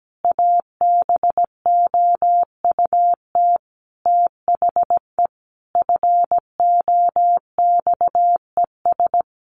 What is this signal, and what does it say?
Morse code, heard as a keyed tone, about 17 wpm, 700 Hz. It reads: ABOUT THE FOXES
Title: A Sound Of Thunder 17wpm